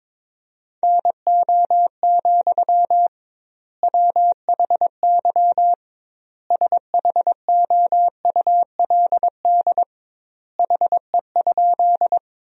Morse code, heard as a keyed tone, about 22 wpm, 700 Hz. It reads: NO, WHY SHOULD HE?
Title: 2x Word-Spacing - Sentences from Top 300 Words - 22wpm